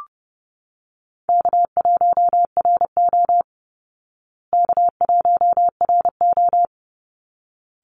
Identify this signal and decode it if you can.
Morse code, heard as a keyed tone, about 30 wpm, 700 Hz. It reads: K1RO K1RO
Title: Easy Call Signs Repeated 2x - 30wpm